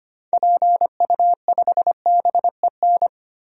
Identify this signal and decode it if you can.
Morse code, heard as a keyed tone, about 25 wpm, 700 Hz. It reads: PU5BEN
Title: DXCC-100 Call Signs - 6 Characters - Repeated 3x - 25wpm